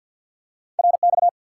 Morse code, heard as a keyed tone, about 49 wpm, 700 Hz. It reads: RX
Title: Top 90 QSO Elements 50wpm